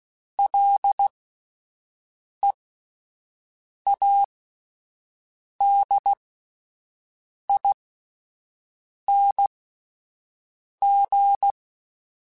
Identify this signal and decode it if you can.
Morse code, heard as a keyed tone, about 16 wpm, spaced as for 5 wpm, 800 Hz. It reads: LEADING
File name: Alphabetical / Alphabetical mp3s / Ls @5